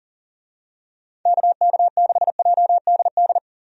Morse code, heard as a keyed tone, about 41 wpm, 700 Hz. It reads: KK6JDD
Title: USA Call Signs - 6 Characters - 40wpm